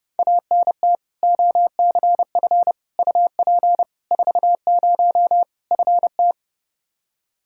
Morse code, heard as a keyed tone, about 30 wpm, 700 Hz. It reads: ANT OCF UP 40 FT
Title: K1USN SST QSO Practice 30wpm